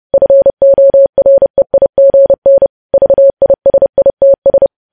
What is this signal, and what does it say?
Morse code, heard as a keyed tone, about 30 wpm, 550 Hz. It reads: FOREIGN VISITS
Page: news headlines in morse code at 30 wpm